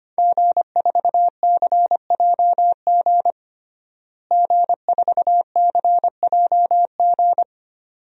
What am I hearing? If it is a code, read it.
Morse code, heard as a keyed tone, about 25 wpm, 700 Hz. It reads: G4CJG G4CJG
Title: DXCC-100 Call Signs - 5 Characters - Repeated 2x - 25wpm